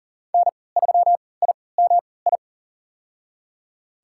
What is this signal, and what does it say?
Morse code, heard as a keyed tone, about 40 wpm, spaced as for 23 wpm, 700 Hz. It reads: N3IMI
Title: USA Call Signs - 5 Characters - 3x Character Spacing - 40wpm